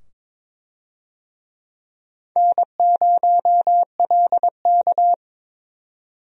Morse code, heard as a keyed tone, about 22 wpm, 700 Hz. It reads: N0LK
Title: USA Call Signs - 4 Characters - 22wpm